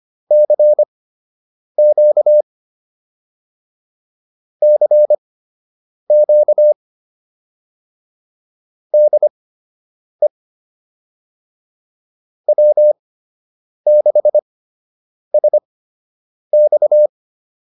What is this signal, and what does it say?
Morse code, heard as a keyed tone, about 25 wpm, spaced as for 8 wpm, 600 Hz. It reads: CQ CQ DE W6SX